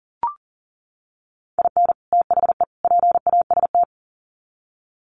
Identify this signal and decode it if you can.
Morse code, heard as a keyed tone, about 40 wpm, 700 Hz. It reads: IN THE PAST
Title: Sets of 3 Words 40wpm